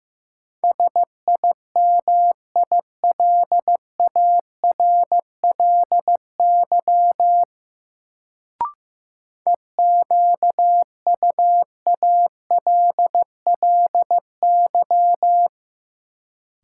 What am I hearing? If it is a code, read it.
Morse code, heard as a keyed tone, about 15 wpm, 700 Hz. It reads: SIMILARLY EQUALLY
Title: Sentences from Top 1400 Words - New Word Drill Alt - 15wpm